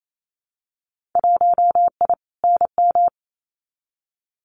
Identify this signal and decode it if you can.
Morse code, heard as a keyed tone, about 28 wpm, 700 Hz. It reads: 1I NM
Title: Winter Field Day Exchanges Encore 28wpm